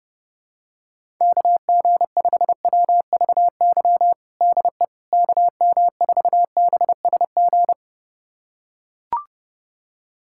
Morse code, heard as a keyed tone, about 32 wpm, 700 Hz. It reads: KG5WVY DE KM4BSG
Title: USA Callsign DE Callsign - 6 Characters - 30wpm